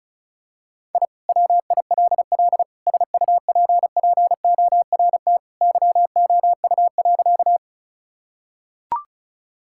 Morse code, heard as a keyed tone, about 34 wpm, 700 Hz. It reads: I WILL SUPPORT YOU.
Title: Sentences from Top 1000 Words - Review Part 5 - 35wpm